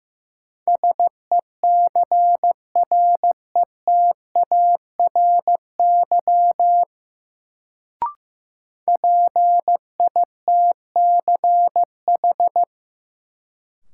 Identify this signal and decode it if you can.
Morse code, heard as a keyed tone, about 16 wpm, 700 Hz. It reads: SECRETARY PITCH
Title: Sentences from Top 1300 Words - New Word Drill Alt - 15wpm